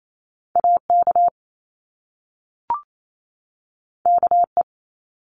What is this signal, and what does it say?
Morse code, heard as a keyed tone, about 28 wpm, 700 Hz. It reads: A K K E
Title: Top Two Letter Combinations in Words 28wpm